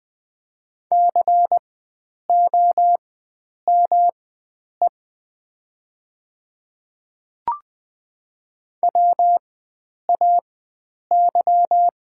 Morse code, heard as a keyed tone, about 20 wpm, spaced as for 9 wpm, 700 Hz. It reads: COME WAY
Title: 4x Character Spacing - Top 100 Words - 20wpm